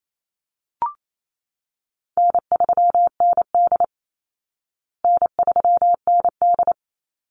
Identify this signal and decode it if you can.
Morse code, heard as a keyed tone, about 28 wpm, 700 Hz. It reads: N3ND N3ND
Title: Easy Call Signs Repeated 3x - 28wpm